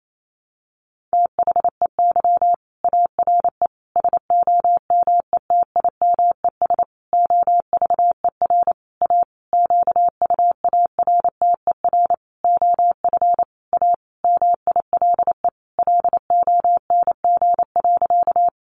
Morse code, heard as a keyed tone, about 28 wpm, 700 Hz. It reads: THEY ARE SOMETIMES OVER A QUARTER OF A MILE LONG.
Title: Sentences from Top 1300 Words 28wpm